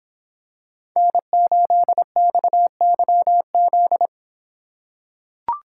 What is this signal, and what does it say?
Morse code, heard as a keyed tone, about 26 wpm, 700 Hz. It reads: N8XYZ